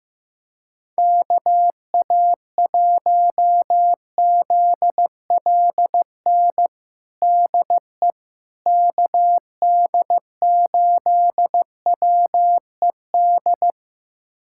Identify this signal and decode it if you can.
Morse code, heard as a keyed tone, about 15 wpm, 700 Hz. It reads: KA1ZLN DE KD8WED
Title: USA Callsign DE Callsign - 6 Characters - 15wpm